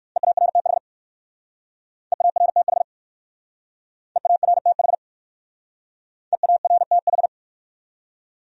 Morse code, heard as a keyed tone, about 55 wpm, 700 Hz. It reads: EARTH EARTH EARTH EARTH
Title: Speed-Racing Top 300 Words 45wpm